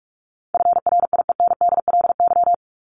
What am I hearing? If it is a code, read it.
Morse code, heard as a keyed tone, about 44 wpm, 700 Hz. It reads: FRIENDLY
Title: Top 1751-2000 Words 45wpm